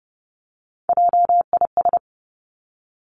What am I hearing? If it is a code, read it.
Morse code, heard as a keyed tone, about 30 wpm, 700 Hz. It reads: JIS
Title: Three Challenging Letter-Number Combinations 30wpm